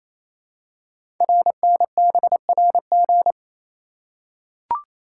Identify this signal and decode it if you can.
Morse code, heard as a keyed tone, about 28 wpm, 700 Hz. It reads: RNBRG
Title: Code Groups - 5 Letters - 28wpm